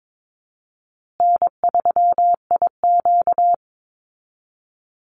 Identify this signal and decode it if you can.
Morse code, heard as a keyed tone, about 22 wpm, 700 Hz. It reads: N3IQ
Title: Easy Call Signs Repeated 2x - 22wpm